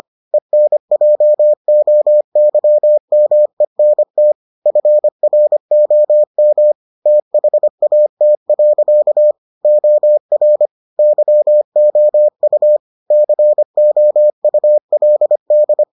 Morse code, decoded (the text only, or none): ENJOYMENT FROM THAT. OR YOU COULD